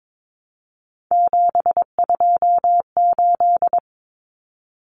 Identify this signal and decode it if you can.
Morse code, heard as a keyed tone, about 22 wpm, 700 Hz. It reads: 728
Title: Common Cut Numbers - 3-Digit 22wpm